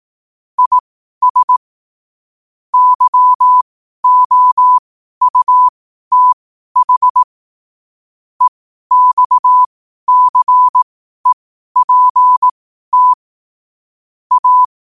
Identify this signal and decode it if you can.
Morse code, heard as a keyed tone, about 18 wpm, spaced as for 12 wpm, 1000 Hz. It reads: IS YOUTH EXCEPT A
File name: qotdfiles / 12wpm